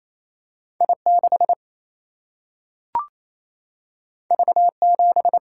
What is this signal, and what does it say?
Morse code, heard as a keyed tone, about 28 wpm, 700 Hz. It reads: I 6 V 7
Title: Single-Letter Single-Number 28wpm